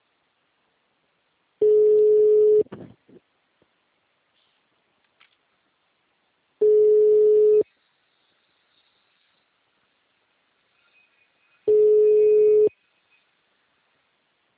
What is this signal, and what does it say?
Ringback tone (Europe, 425 Hz), with speech.